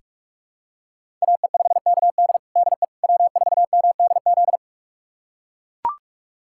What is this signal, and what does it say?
Morse code, heard as a keyed tone, about 41 wpm, 700 Hz. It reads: AE5KD DE W4MDZ
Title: USA Callsign DE Callsign - 5 Characters - 45wpm